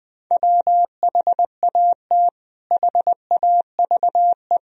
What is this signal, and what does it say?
Morse code, heard as a keyed tone, about 20 wpm, 700 Hz. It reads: WHAT HAVE
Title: Sentences from Top 1000 Words - Review Part 3 - 20wpm